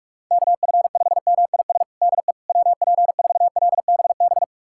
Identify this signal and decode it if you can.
Morse code, heard as a keyed tone, about 46 wpm, 700 Hz. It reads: KF5MIS DE WP4LBB
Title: USA Callsign DE Callsign - 6 Characters - 45wpm